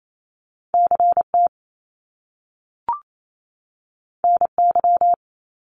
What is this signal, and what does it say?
Morse code, heard as a keyed tone, about 28 wpm, 700 Hz. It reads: CT NY
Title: Top Two Letter Combinations in Words 28wpm